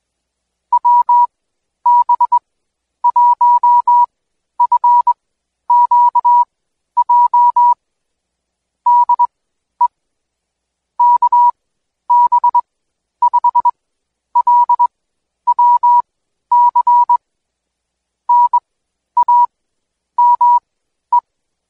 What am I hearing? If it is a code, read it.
Morse code, heard as a keyed tone, about 22 wpm, spaced as for 11 wpm, 970 Hz. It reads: WB1FQJ DE KB5LWC NAME
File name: Code Quick / AudioDownld2 / Seg120-Day 25-DayDrill 25